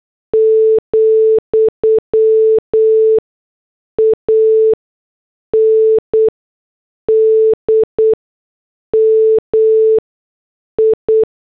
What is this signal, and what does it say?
Morse code, heard as a keyed tone, about 8 wpm, spaced as for 6 wpm, 440 Hz. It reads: , AND MI
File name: morse / headlines